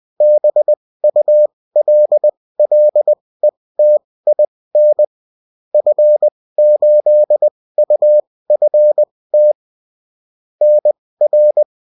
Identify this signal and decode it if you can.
Morse code, heard as a keyed tone, about 20 wpm, spaced as for 16 wpm, 600 Hz. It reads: BULLETIN F8UFT NR